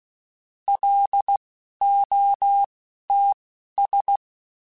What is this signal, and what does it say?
Morse code, heard as a keyed tone, about 16 wpm, spaced as for 12 wpm, 800 Hz. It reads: LOTS
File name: Spaced Characters / Spaced Characters-mp3s / Weather 5 12 sp @15